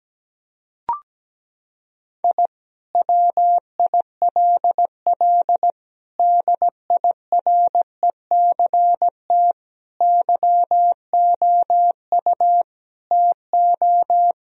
Morse code, heard as a keyed tone, about 17 wpm, 700 Hz. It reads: I WILL DIRECT YOU TO
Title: Sentences from Top 1000 Words - Review Part 2 - 17wpm